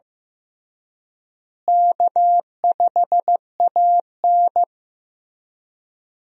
Morse code, heard as a keyed tone, about 15 wpm, 700 Hz. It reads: K5AN
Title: USA Call Signs - 4 Characters - Repeated 2x - 15wpm